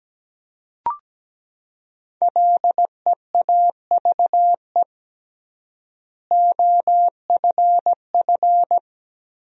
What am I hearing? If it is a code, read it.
Morse code, heard as a keyed tone, about 17 wpm, 700 Hz. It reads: LEAVE OFF
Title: 3x Word-Spacing - Sentences from Top 400 Words - 17wpm